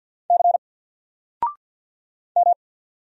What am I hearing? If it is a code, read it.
Morse code, heard as a keyed tone, about 48 wpm, 700 Hz. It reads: C M
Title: Single Character (Letter:Number:Punctuation:Prosign) 50wpm